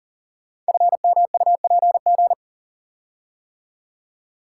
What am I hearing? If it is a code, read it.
Morse code, heard as a keyed tone, about 43 wpm, 700 Hz. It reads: FMUPG
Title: Code Groups - 5 Letters - 40wpm